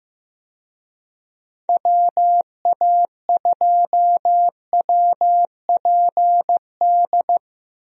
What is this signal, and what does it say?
Morse code, heard as a keyed tone, about 15 wpm, 700 Hz. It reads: WA2WPD
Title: USA Call Signs - 6 Characters - Repeated 2x - 15wpm